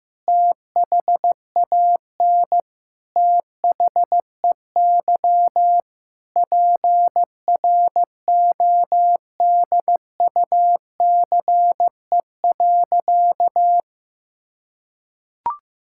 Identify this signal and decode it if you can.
Morse code, heard as a keyed tone, about 15 wpm, 700 Hz. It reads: THAN THEY PRODUCE.
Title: Sentences from Top 1000 Words - Review Part 5 - 15wpm